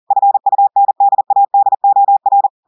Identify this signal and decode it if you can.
Morse code, heard as a keyed tone, about 40 wpm, 800 Hz. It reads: FUNDADOR
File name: audios / portugais / portugais texte en clair 1 vitesse 40 mots minutes